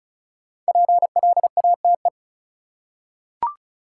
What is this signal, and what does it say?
Morse code, heard as a keyed tone, about 36 wpm, 700 Hz. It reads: PLATE
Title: Five Letter Words - 35wpm